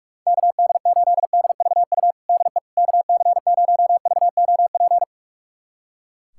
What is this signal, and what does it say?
Morse code, heard as a keyed tone, about 42 wpm, 700 Hz. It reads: KD8DVU DE KC0VOP
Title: USA Callsign DE Callsign 45wpm